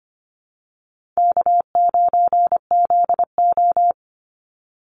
Morse code, heard as a keyed tone, about 25 wpm, 700 Hz. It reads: K9ZO
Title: Easy Call Signs - 25wpm